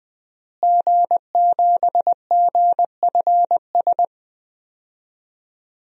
Morse code, heard as a keyed tone, about 20 wpm, 700 Hz. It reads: G7GFS
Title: DXCC-100 Call Signs - 5 Characters - Repeated 2x - 20wpm